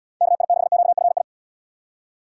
Morse code, heard as a keyed tone, about 70 wpm, 700 Hz. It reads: KI6QFI